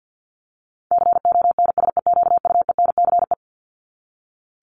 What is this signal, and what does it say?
Morse code, heard as a keyed tone, about 50 wpm, 700 Hz. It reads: CONSEQUENCE
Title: LY Suffix - Top 100 Words with Base Words 50wpm